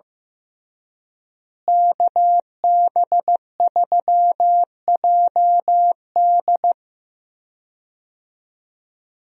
Morse code, heard as a keyed tone, about 15 wpm, 700 Hz. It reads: KB3JD
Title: USA Call Signs - 5 Characters - Repeated 3x - 15wpm